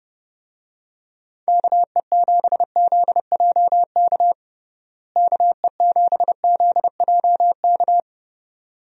Morse code, heard as a keyed tone, about 30 wpm, 700 Hz. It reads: KE7ZJK KE7ZJK
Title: USA Call Signs - 6 Characters - Repeated 2x - 30wpm